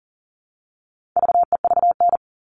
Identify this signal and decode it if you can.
Morse code, heard as a keyed tone, about 40 wpm, 700 Hz. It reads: VEVN
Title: All Cut Numbers - 4-Digit 40wpm